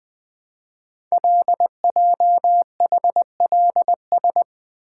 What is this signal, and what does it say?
Morse code, heard as a keyed tone, about 20 wpm, 700 Hz. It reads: LJHLS
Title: Code Groups - 5 Letters - 20wpm